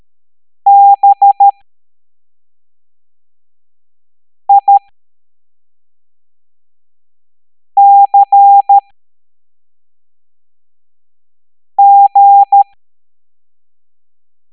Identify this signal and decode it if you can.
Morse code, heard as a keyed tone, about 13 wpm, spaced as for 3 wpm, 800 Hz. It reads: BICG